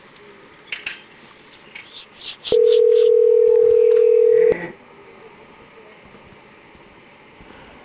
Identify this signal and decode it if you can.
Ringback tone (North America), 440 + 480 Hz.